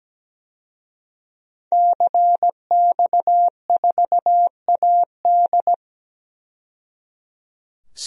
Morse code, heard as a keyed tone, about 17 wpm, 700 Hz. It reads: CX4AD
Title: DXCC-100 Call Signs - 5 Characters - 17wpm